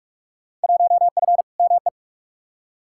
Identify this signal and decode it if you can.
Morse code, heard as a keyed tone, about 46 wpm, 700 Hz. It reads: 1F ME
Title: Field Day Exchanges Encore - 45wpm